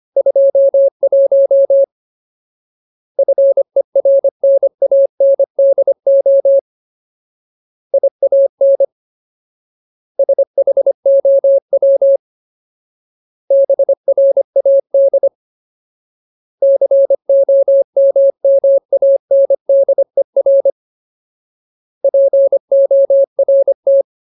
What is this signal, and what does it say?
Morse code, heard as a keyed tone, about 25 wpm, 550 Hz. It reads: 21 FERNANDO IAN SHOW BRAD COMMANDER PORT
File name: Ultimate Word List / 25WPM Ultimate